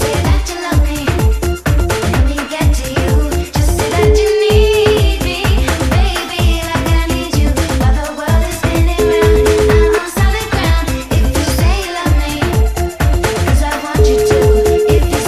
Ringback tone (Europe, 425 Hz), with noise.